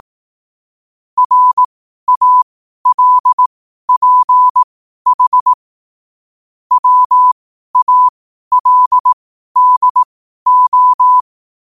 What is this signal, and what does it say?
Morse code, heard as a keyed tone, about 18 wpm, spaced as for 12 wpm, 1000 Hz. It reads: RALPH WALDO
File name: qotdfiles / 12wpm